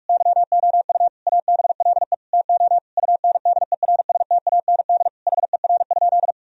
Morse code, heard as a keyed tone, about 46 wpm, 700 Hz. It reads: YOU ABLE TO UNDERSTAND HER?
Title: Sentences from Top 1000 Words - Review Part 2 - 45wpm